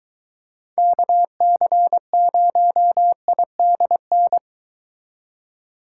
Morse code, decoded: KC0IDN

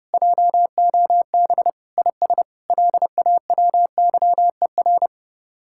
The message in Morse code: JOB IS LAWYER

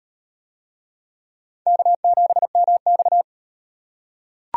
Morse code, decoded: K7MX

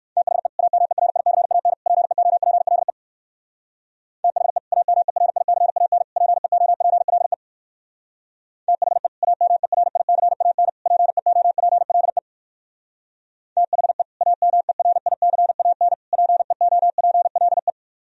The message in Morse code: THE AMERICAN PEOPLE THE AMERICAN PEOPLE THE AMERICAN PEOPLE THE AMERICAN PEOPLE

